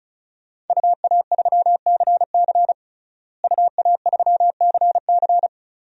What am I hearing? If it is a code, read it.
Morse code, heard as a keyed tone, about 36 wpm, 700 Hz. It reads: UA3CC UA3CC
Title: DXCC-100 Call Signs - 5 Characters - Repeated 2x - 35wpm